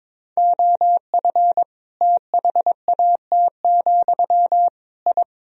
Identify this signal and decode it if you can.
Morse code, heard as a keyed tone, about 22 wpm, 700 Hz. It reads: OF THAT, I
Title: Sentences from Top 1000 Words - Review Part 2 - 22wpm